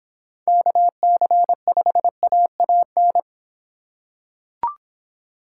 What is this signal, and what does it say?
Morse code, heard as a keyed tone, about 25 wpm, 700 Hz. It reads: KC5AAN